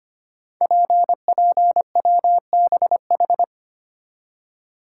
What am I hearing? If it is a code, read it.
Morse code, heard as a keyed tone, about 25 wpm, 700 Hz. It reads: PPWBH
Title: Code Groups - 5 Letters - 25wpm